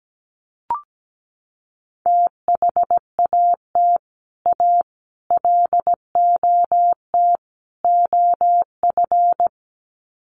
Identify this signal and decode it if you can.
Morse code, heard as a keyed tone, about 17 wpm, 700 Hz. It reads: THAT A LOT OF
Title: Sets of 4 Words 17wpm